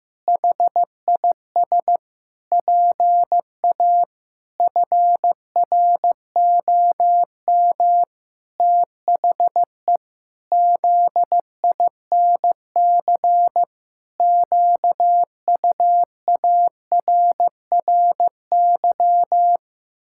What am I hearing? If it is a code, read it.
Morse code, heard as a keyed tone, about 15 wpm, 700 Hz. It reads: HIS PA FROM THE ZINC QUARRY
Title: Pangram Challenge v2 - 15wpm